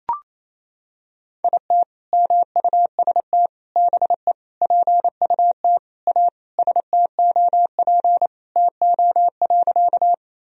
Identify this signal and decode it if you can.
Morse code, heard as a keyed tone, about 28 wpm, 700 Hz. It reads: IT MUST BE PUT A STOP TO.
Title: Sentences from Top 1000 Words - Review Part 1 - 28wpm